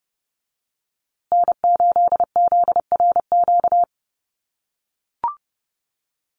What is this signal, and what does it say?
Morse code, heard as a keyed tone, about 30 wpm, 700 Hz. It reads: N8ZRQ
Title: USA Call Signs Repeated 3x 30wpm